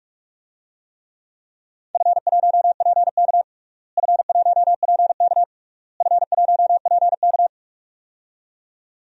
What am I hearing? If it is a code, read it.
Morse code, heard as a keyed tone, about 42 wpm, 700 Hz. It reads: F1PK F1PK F1PK
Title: DXCC-100 Call Signs - 4 Characters - Repeated 3x - 45wpm